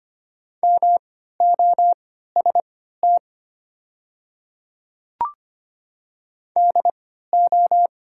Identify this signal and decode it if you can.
Morse code, heard as a keyed tone, about 25 wpm, spaced as for 14 wpm, 700 Hz. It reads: MOST DO
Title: 3x Character Spacing - Top 100 Words - 25wpm